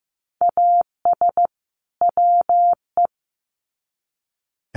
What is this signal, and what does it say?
Morse code, heard as a keyed tone, about 15 wpm, 700 Hz. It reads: AS WE
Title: Sets of 2 Words 15wpm